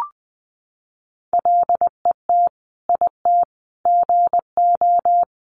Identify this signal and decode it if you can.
Morse code, heard as a keyed tone, about 19 wpm, 700 Hz. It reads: LET IT GO
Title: Sentences from Top 1200 Words 20wpm